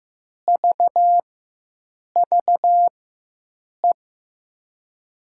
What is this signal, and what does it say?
Morse code, heard as a keyed tone, about 15 wpm, spaced as for 7 wpm, 700 Hz. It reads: VVE
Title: USA Call Signs - 6 Characters - 4x Character Spacing - 15wpm